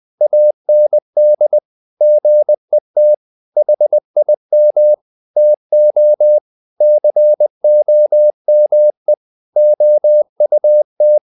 Morse code, heard as a keyed tone, about 20 wpm, 600 Hz. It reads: AND GET HIM TO COME OUT
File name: the old man and the sea / the old man and the sea part 1 20wpm